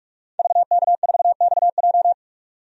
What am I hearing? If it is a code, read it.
Morse code, heard as a keyed tone, about 47 wpm, 700 Hz. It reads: VK4XJ
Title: DXCC-100 Call Signs - 5 Characters - 45wpm